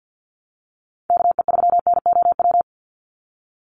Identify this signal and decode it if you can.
Morse code, heard as a keyed tone, about 50 wpm, 700 Hz. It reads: KE3NOW
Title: USA Call Signs 50wpm